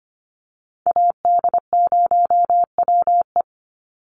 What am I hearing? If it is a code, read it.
Morse code, heard as a keyed tone, about 25 wpm, 700 Hz. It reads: AD0WE
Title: Call Signs Repeated 3x - 25wpm